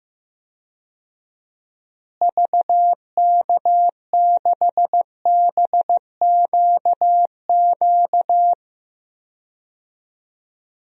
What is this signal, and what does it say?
Morse code, heard as a keyed tone, about 15 wpm, 700 Hz. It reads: VK6BQQ
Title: DXCC-100 Call Signs - 6 Characters - Repeated 2x - 15wpm